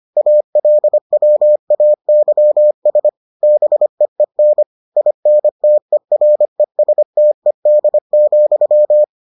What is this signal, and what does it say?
Morse code, heard as a keyed tone, about 25 wpm, 600 Hz. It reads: ALWAYS BEEN INTERESTED,